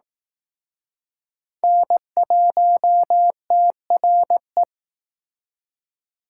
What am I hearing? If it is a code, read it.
Morse code, heard as a keyed tone, about 18 wpm, 700 Hz. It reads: N1TRE
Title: Speed Racing - USA 1x3 Call Signs 15wpm